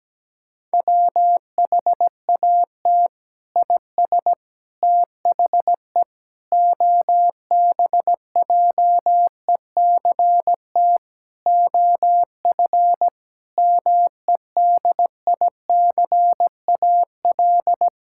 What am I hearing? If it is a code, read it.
Morse code, heard as a keyed tone, about 17 wpm, 700 Hz. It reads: WHAT IS THE OBJECT OF MEDICAL
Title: Sentences from Top 1000 Words - Review Part 6 - 17wpm